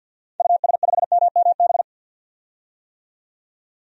Morse code, heard as a keyed tone, about 51 wpm, 700 Hz. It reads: US5MMB
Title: DXCC-100 Call Signs - 6 Characters - 50wpm